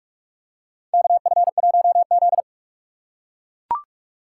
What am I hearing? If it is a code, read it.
Morse code, heard as a keyed tone, about 43 wpm, 700 Hz. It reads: KF1Z
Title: USA Call Signs - 4 Characters - 6x Character Spacing - 45wpm